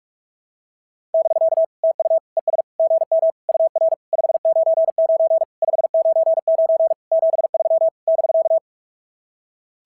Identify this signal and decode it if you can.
Morse code, heard as a keyed tone, about 44 wpm, 690 Hz. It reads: <BK> TU ES GM UR 599 599 73 <BK>
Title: SOTA QSO Practice 45wpm